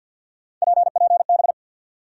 Morse code, heard as a keyed tone, about 53 wpm, 700 Hz. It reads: PPB
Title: USA Call Sign Postfixes - 3 Letters 50wpm